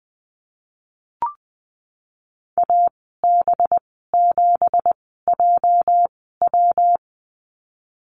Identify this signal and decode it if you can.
Morse code, heard as a keyed tone, about 20 wpm, spaced as for 15 wpm, 700 Hz. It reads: AB7JW
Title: 238 Lesson 38g - Letter q - Call Signs 20wpm